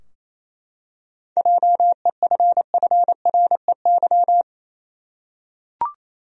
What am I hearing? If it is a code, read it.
Morse code, heard as a keyed tone, about 28 wpm, 700 Hz. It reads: JEFFREY
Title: USA Common Names - 28wpm